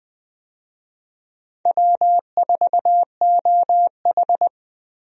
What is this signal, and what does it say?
Morse code, heard as a keyed tone, about 20 wpm, 700 Hz. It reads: W4OH
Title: USA Call Signs - 4 Characters - Repeated 3x - 20wpm